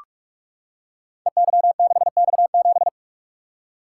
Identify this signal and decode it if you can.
Morse code, heard as a keyed tone, about 44 wpm, 700 Hz. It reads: EY6X7
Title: Code Groups - 5 Letter:Numbers - 45wpm